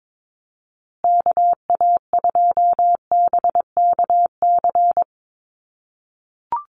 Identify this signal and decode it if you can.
Morse code, heard as a keyed tone, about 22 wpm, 690 Hz. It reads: KA2BKC
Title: USA Call Signs Repeated 3x 22wpm